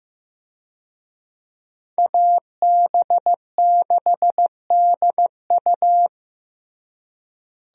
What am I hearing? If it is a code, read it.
Morse code, heard as a keyed tone, about 15 wpm, 700 Hz. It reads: AB6DU
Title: USA Call Signs - 5 Characters - Repeated 2x - 15wpm